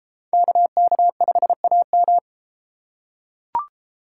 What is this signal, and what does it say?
Morse code, heard as a keyed tone, about 32 wpm, 700 Hz. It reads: KK5AM